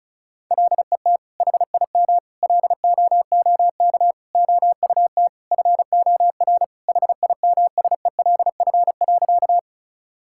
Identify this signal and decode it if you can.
Morse code, heard as a keyed tone, about 36 wpm, 700 Hz. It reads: LET HIM LOOK OUT FOR HIMSELF.
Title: Sentences from Top 1600 Words 35wpm